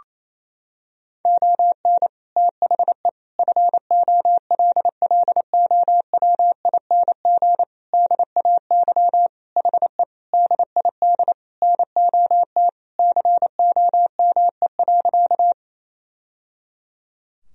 Morse code, heard as a keyed tone, about 28 wpm, 700 Hz. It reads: ON THE FOLLOWING DAY HE DID NOT COME.
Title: Sentences from Top 1000 Words - Review Part 4 - 28wpm